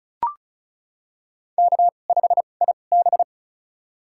Morse code, heard as a keyed tone, about 34 wpm, spaced as for 25 wpm, 700 Hz. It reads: K5IB
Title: USA Call Signs - 4 Characters - 2x Character Spacing - 35wpm